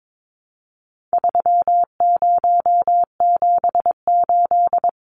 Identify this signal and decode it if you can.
Morse code, heard as a keyed tone, about 22 wpm, 700 Hz. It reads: 3078
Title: Numbers - 4-Digit 22wpm